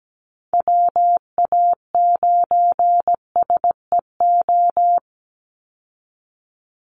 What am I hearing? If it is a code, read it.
Morse code, heard as a keyed tone, about 17 wpm, 700 Hz. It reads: WA9SEO
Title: Call Signs Repeated 3x - 17wpm